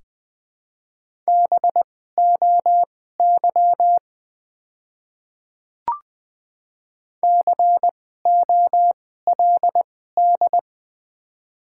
Morse code, heard as a keyed tone, about 20 wpm, spaced as for 14 wpm, 700 Hz. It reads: BOY COLD
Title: 2x Character Spacing - Top 500 Words - 20wpm